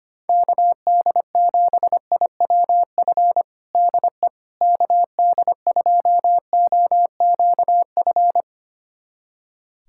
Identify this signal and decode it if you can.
Morse code, heard as a keyed tone, about 25 wpm, 700 Hz. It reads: KD7IWF DE KD2OQF
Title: USA Callsign DE Callsign - 6 Characters - 25wpm